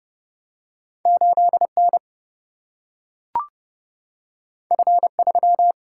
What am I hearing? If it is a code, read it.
Morse code, heard as a keyed tone, about 30 wpm, 700 Hz. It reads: 8N F3
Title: Single-Letter Single-Number 30wpm